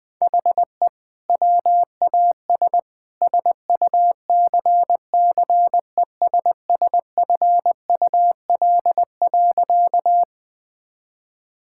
Morse code, heard as a keyed tone, about 21 wpm, 700 Hz. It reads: HE WAS SUCCESSFUL.
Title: Sentences from Top 1000 Words - Review Part 6 - 20wpm